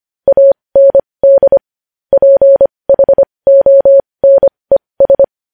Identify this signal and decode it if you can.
Morse code, heard as a keyed tone, about 25 wpm, spaced as for 20 wpm, 550 Hz. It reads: AND PHONES
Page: news headlines in morse code at 20 wpm